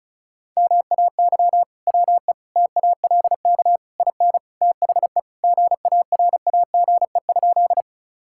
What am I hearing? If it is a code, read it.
Morse code, heard as a keyed tone, about 34 wpm, 700 Hz. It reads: MAY WE TALK IN THE GARAGE?
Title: Sentences from Top 1600 Words 35wpm